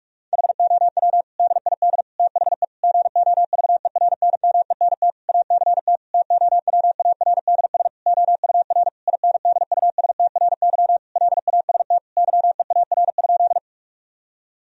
Morse code, heard as a keyed tone, about 41 wpm, 700 Hz. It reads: HOW DID THE GOVERNMENT ACT TOWARDS OUR INDUSTRY LAST YEAR?
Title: Sentences from Top 1000 Words - Review Part 4 - 45wpm